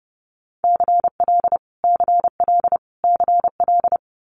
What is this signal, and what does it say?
Morse code, heard as a keyed tone, about 30 wpm, 700 Hz. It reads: CL CL CL
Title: Multi-letter Phonemes Repeated 3x 30wpm